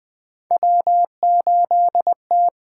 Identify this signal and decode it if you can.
Morse code, heard as a keyed tone, about 20 wpm, 700 Hz. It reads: W8T
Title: USA Call Signs - 8x Character Spacing - 20wpm